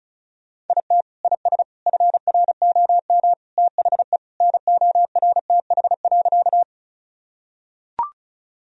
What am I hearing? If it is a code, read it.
Morse code, heard as a keyed tone, about 34 wpm, 700 Hz. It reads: IT IS FROM THE NORTH.
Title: Sentences from Top 1000 Words - Review Part 1 - 35wpm